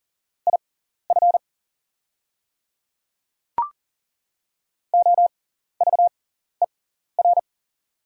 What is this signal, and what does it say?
Morse code, heard as a keyed tone, about 39 wpm, spaced as for 14 wpm, 700 Hz. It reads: IF OVER
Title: 6x Character Spacing - Top 200 Words - 40wpm